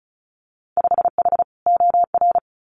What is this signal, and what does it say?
Morse code, heard as a keyed tone, about 34 wpm, 700 Hz. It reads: 5H OR
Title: Winter Field Day Exchanges Encore 35wpm